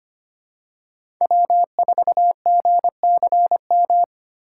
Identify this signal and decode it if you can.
Morse code, heard as a keyed tone, about 25 wpm, 700 Hz. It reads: W4GCM
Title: USA Call Signs - 5 Characters - 3x Character Spacing - 25wpm